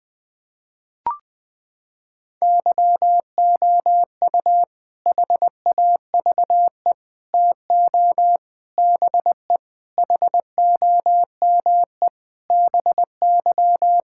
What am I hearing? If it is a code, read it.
Morse code, heard as a keyed tone, about 20 wpm, 700 Hz. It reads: YOU HAVE TO BE HOME BY
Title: Sentences with 4-Word Phrases - 20wpm